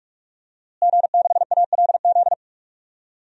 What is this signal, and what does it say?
Morse code, heard as a keyed tone, about 46 wpm, 700 Hz. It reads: G6ALZ
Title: DXCC-100 Call Signs - 5 Characters - Repeated 2x - 45wpm